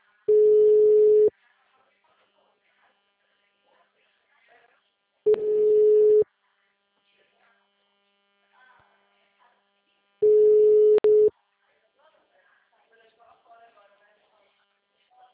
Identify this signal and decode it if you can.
Ringback tone (Europe, 425 Hz), with speech.
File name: out- -2010- - - .46253